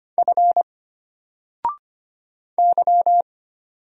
Morse code, heard as a keyed tone, about 25 wpm, 700 Hz. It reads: F Y
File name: Single-Letter-Number-25wpm